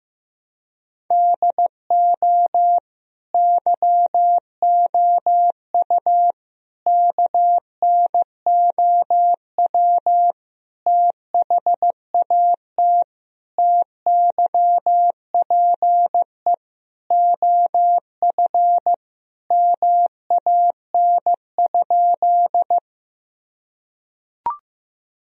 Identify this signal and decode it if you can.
Morse code, heard as a keyed tone, about 16 wpm, 700 Hz. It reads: DO YOU KNOW THAT TYPE OF MAN?
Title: Sentences from Top 1000 Words - Review Part 3 - 15wpm